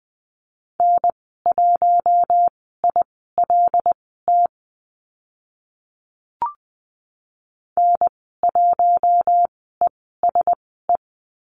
Morse code, heard as a keyed tone, about 20 wpm, spaced as for 15 wpm, 700 Hz. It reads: N1ILT N1ESE
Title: 076 Lesson 13g - Letter L - Call Signs 20wpm